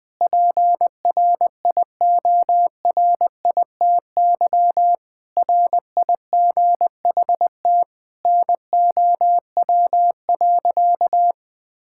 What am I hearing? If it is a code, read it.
Morse code, heard as a keyed tone, about 20 wpm, 700 Hz. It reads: PRIORITY RIGHT NOW.
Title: Sentences from Top 1600 Words 20wpm